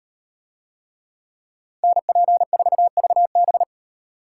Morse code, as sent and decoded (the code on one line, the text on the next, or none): -. .--. ....- ...- -...
NP4VB